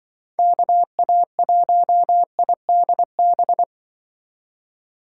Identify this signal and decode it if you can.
Morse code, heard as a keyed tone, about 24 wpm, 700 Hz. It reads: KA1IDB
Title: Speed Racing - USA Call Signs 20wpm